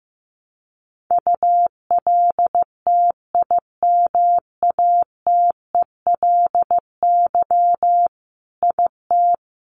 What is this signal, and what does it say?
Morse code, heard as a keyed tone, about 15 wpm, 700 Hz. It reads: ULTIMATELY IT
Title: Sentences from Top 1200 Words 15wpm